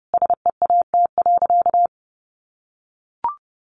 Morse code, heard as a keyed tone, about 30 wpm, 700 Hz. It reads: SEAT.
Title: Sentences from Top 1500 Words 30wpm